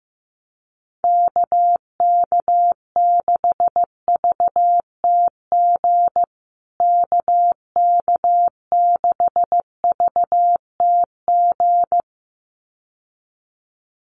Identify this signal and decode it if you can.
Morse code, heard as a keyed tone, about 15 wpm, 700 Hz. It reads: KK6VTG KK6VTG
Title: USA Call Signs Repeated 2x 15wpm